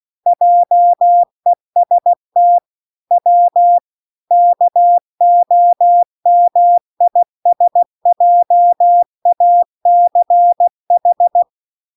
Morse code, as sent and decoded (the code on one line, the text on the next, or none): .--- . ... - / .-- / -.- --- -- .. ... .--- .- -.-. ....
JEST W KOMISJACH